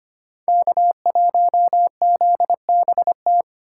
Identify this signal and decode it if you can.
Morse code, heard as a keyed tone, about 25 wpm, 700 Hz. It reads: K1ZBT
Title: USA Call Signs - 5 Characters - 2x Character Spacing - 25wpm